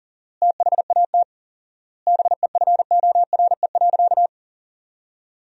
Morse code, heard as a keyed tone, about 37 wpm, 700 Hz. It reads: THAT BEFORE.
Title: 4x Word-Spacing - Sentences from Top 400 Words - 40wpm